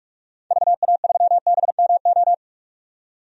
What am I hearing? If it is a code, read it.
Morse code, heard as a keyed tone, about 46 wpm, 700 Hz. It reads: VA3BMO
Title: DXCC-100 Call Signs - 6 Characters - 45wpm